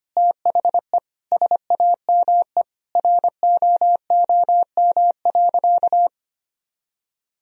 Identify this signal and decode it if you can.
Morse code, heard as a keyed tone, about 25 wpm, 700 Hz. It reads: THE SAME ROOM.
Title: Sentences from Top 1000 Words - Review Part 3 - 25wpm